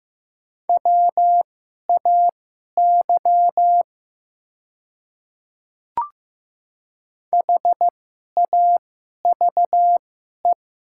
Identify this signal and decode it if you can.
Morse code, heard as a keyed tone, about 15 wpm, spaced as for 11 wpm, 700 Hz. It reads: WAY HAVE
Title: 2x Character Spacing - Top 200 Words - 15wpm